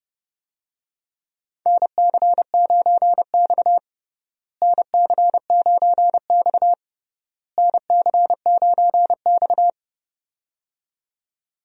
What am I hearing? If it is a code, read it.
Morse code, heard as a keyed tone, about 30 wpm, 700 Hz. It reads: NC9X NC9X NC9X
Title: USA Call Signs - 4 Characters - Repeated 3x - 30wpm